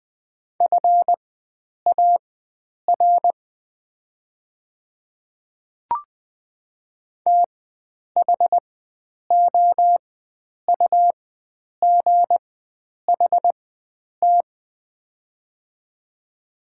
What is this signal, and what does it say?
Morse code, heard as a keyed tone, about 20 wpm, spaced as for 9 wpm, 700 Hz. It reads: FAR THOUGHT
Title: 4x Character Spacing - Top 500 Words - 20wpm